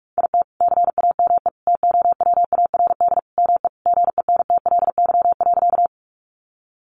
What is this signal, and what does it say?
Morse code, heard as a keyed tone, about 44 wpm, 700 Hz. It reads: IT CAME TOWARD ME GENTLY.
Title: Sentences from Top 1500 Words 45wpm